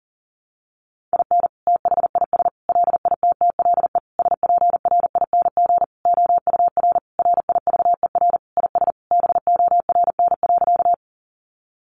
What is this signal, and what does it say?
Morse code, heard as a keyed tone, about 40 wpm, 700 Hz. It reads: IN THIS LITTLE SPRING OUR RIVER IS BORN.
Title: Sentences from Top 1200 Words 40wpm